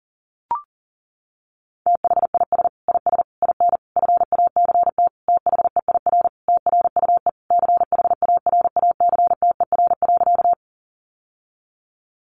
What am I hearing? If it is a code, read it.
Morse code, heard as a keyed tone, about 40 wpm, 700 Hz. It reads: THIS IS IN FACT THEIR TRUE CHARACTER.
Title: Sentences from Top 1200 Words 40wpm